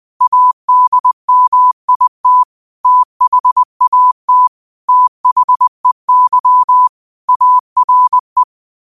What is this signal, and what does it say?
Morse code, heard as a keyed tone, about 20 wpm, 1000 Hz. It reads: ADMIT THAT THEY ARE
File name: qotdfiles / 20wpm